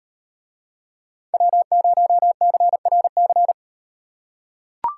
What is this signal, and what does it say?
Morse code, heard as a keyed tone, about 38 wpm, 700 Hz. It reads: W0CRC